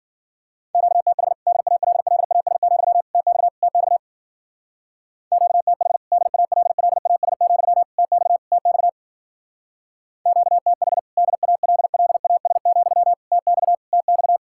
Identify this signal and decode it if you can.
Morse code, heard as a keyed tone, about 50 wpm, 700 Hz. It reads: QTH DALLAS, TX TX QTH DALLAS, TX TX QTH DALLAS, TX TX